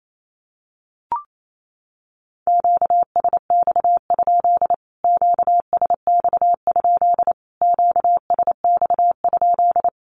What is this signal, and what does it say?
Morse code, heard as a keyed tone, about 28 wpm, 700 Hz. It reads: QSX? QSX? QSX?
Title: Q-Codes Repeated 3x 28wpm